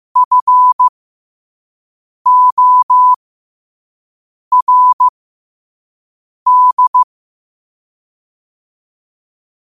Morse code, heard as a keyed tone, about 15 wpm, spaced as for 5 wpm, 1000 Hz. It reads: FORD
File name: qotdfiles / 5wpm